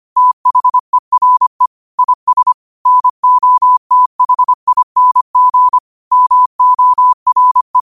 Morse code, heard as a keyed tone, about 25 wpm, 1000 Hz. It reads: THERE IS NOTHING MORE
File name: qotdfiles / 25wpm